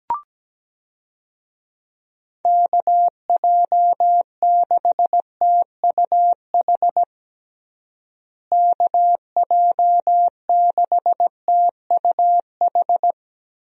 Morse code, heard as a keyed tone, about 17 wpm, 700 Hz. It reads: KJ6TUH KJ6TUH
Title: USA Call Signs - 6 Characters - Repeated 3x - 17wpm